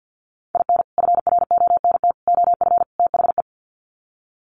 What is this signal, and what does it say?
Morse code, heard as a keyed tone, about 50 wpm, 700 Hz. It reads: IN FRONT OF THE
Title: Sets of 4 Words 50wpm